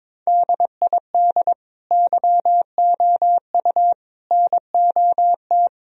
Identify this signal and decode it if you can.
Morse code, heard as a keyed tone, about 22 wpm, 700 Hz. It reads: DID YOU NOT
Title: Sentences from Top 1000 Words - Review Part 2 - 22wpm